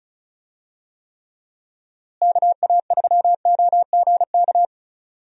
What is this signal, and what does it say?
Morse code, heard as a keyed tone, about 36 wpm, 700 Hz. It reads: KA3OGK
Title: USA Call Signs - 6 Characters - Repeated 3x - 35wpm